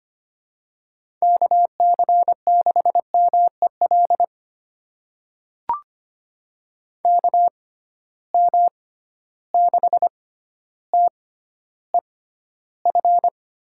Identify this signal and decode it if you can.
Morse code, heard as a keyed tone, about 25 wpm, spaced as for 13 wpm, 700 Hz. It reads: KC6MEL KM6TEF
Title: USA Call Signs - 6 Characters - 6x Character Spacing - 25wpm